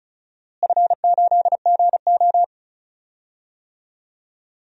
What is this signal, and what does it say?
Morse code, heard as a keyed tone, about 36 wpm, 700 Hz. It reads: F8GO
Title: DXCC-100 Call Signs - 4 Characters - Repeated 3x - 35wpm